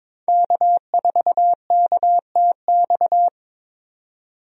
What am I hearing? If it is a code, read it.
Morse code, heard as a keyed tone, about 21 wpm, 700 Hz. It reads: K4KTX
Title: USA Call Signs - 5 Characters - 22wpm